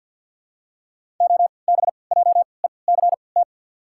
Morse code, heard as a keyed tone, about 53 wpm, spaced as for 29 wpm, 700 Hz. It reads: OBJECT
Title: 3x Character Spacing - Top 500 Words - 50wpm